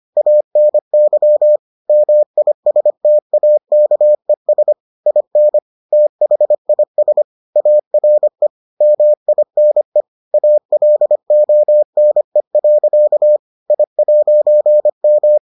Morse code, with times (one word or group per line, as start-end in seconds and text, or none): 0.2-1.6 ANY
1.9-4.7 MISTAKES
5.1-5.6 IN
5.9-7.2 THIS
7.6-8.5 ARE
8.8-10.0 MINE
10.3-13.4 ALONE.
13.7-15.4 I'M